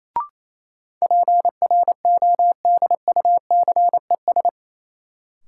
Morse code, heard as a keyed tone, about 27 wpm, 700 Hz. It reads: PRODUCES
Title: Sentences from Top 1000 Words - Review Part 6 - New Word Drill - 28wpm